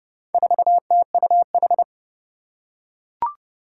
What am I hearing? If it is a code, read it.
Morse code, heard as a keyed tone, about 30 wpm, 700 Hz. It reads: 4TUH
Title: Four Challenging Letter-Number Combinations 30wpm